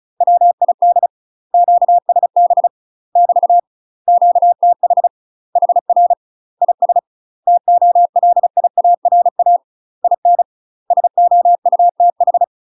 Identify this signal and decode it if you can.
Morse code, heard as a keyed tone, about 34 wpm, 700 Hz. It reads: WID QSB <BT> QTH HR IS TOLIARA IN SOUTH
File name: UK MASTER QSOs TRAINING / 35WPM / UK MASTER QSOs 0043